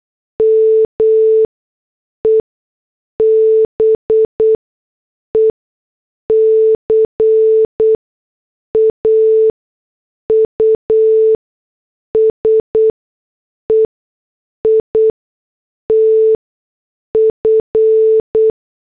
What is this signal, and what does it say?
Morse code, heard as a keyed tone, about 8 wpm, spaced as for 6 wpm, 440 Hz. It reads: ME BECAUSE IT F
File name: morse / headlines